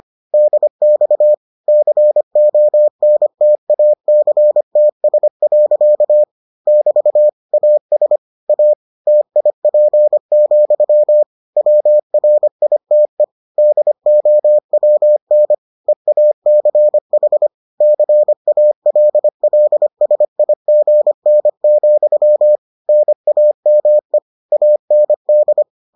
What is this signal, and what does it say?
Morse code, heard as a keyed tone, about 25 wpm, 600 Hz. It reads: DX CONTACTS. <BT> AS A TIP, WRITE DOWN EACH CALLSIGN, NAME AND